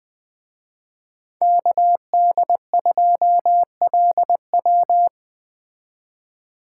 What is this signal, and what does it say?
Morse code, heard as a keyed tone, about 20 wpm, 700 Hz. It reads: KD2LW